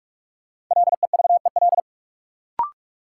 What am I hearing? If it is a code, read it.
Morse code, heard as a keyed tone, about 46 wpm, 700 Hz. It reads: LEVEL
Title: Sentences from Top 700 Words - New Word Drill - 45wpm